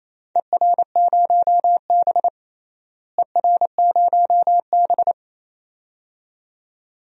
Morse code, heard as a keyed tone, about 28 wpm, 700 Hz. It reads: ER0B ER0B
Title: DXCC-100 Call Signs - 4 Characters - Repeated 2x - 28wpm